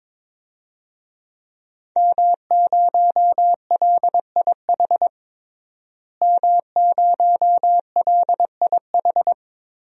Morse code, heard as a keyed tone, about 22 wpm, 700 Hz. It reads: M0LIH M0LIH
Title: DXCC-100 Call Signs - 5 Characters - Repeated 2x - 22wpm